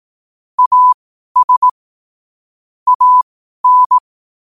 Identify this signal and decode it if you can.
Morse code, heard as a keyed tone, about 18 wpm, spaced as for 12 wpm, 1000 Hz. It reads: AS AN
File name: qotdfiles / 12wpm